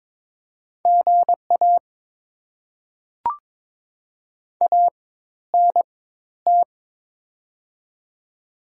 Morse code, heard as a keyed tone, about 22 wpm, spaced as for 13 wpm, 690 Hz. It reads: GA ANT
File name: Top 20 QSO Elements - 4x Character Spacing - 22wpm